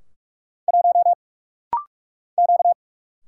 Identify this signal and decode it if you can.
Morse code, heard as a keyed tone, about 44 wpm, 700 Hz. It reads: 1 Q
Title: Single Letter-Number Alt - Rapid-Fire 45wpm